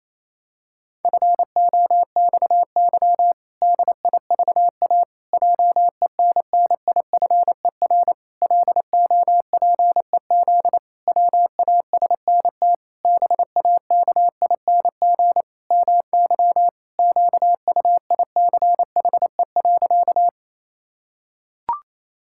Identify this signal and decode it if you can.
Morse code, heard as a keyed tone, about 28 wpm, 700 Hz. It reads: FOXY DIVA JENNIFER LOPEZ WASNT BAKING MY QUICHE.
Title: Pangram Challenge v2 - 28wpm